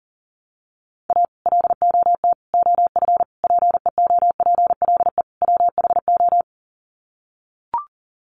Morse code, heard as a keyed tone, about 40 wpm, 700 Hz. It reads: A LOT OF PEOPLE WHO
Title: Sets of 5 Words 40wpm